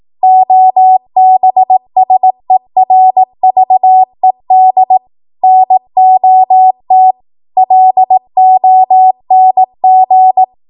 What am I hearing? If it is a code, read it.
Morse code, heard as a keyed tone, about 18 wpm, 750 Hz. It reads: OBSERVED NOT LONG
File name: Morse / 18WPM